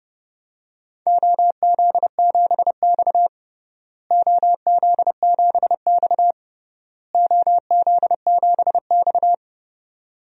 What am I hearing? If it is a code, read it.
Morse code, heard as a keyed tone, about 29 wpm, 700 Hz. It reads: OZ7X OZ7X OZ7X
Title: DXCC-100 Call Signs - 4 Characters - Repeated 3x - 30wpm